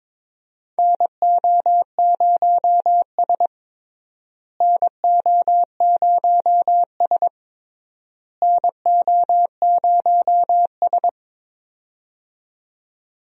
Morse code, heard as a keyed tone, about 22 wpm, 700 Hz. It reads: NO0S NO0S NO0S
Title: USA Call Signs - 4 Characters - Repeated 3x - 22wpm